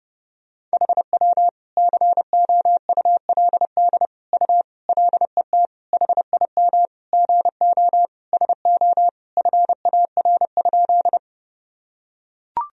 Morse code, heard as a keyed tone, about 32 wpm, 700 Hz. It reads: HW COULD U LET HIM GO SO FAR?
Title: Sentences from Top 300 Words and Abbreviations 30wpm